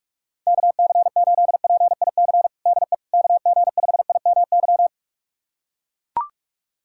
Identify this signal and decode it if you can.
Morse code, heard as a keyed tone, about 41 wpm, 700 Hz. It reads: KC8PIC DE KG5IMY
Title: USA Callsign DE Callsign - 6 Characters - 45wpm